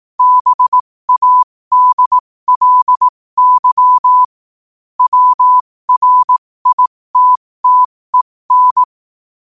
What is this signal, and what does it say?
Morse code, heard as a keyed tone, about 18 wpm, spaced as for 15 wpm, 1000 Hz. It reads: BADLY WRITTEN